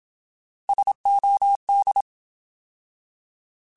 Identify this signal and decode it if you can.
Morse code, heard as a keyed tone, about 27 wpm, 800 Hz. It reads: SOD
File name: Alphabetical / Alphabetical mp3s / Ss @25